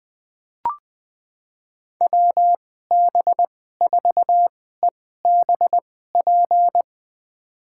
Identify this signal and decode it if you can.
Morse code, heard as a keyed tone, about 20 wpm, spaced as for 15 wpm, 700 Hz. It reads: WB4EBP
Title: USA Call Signs - 2x Character Spacing - 20wpm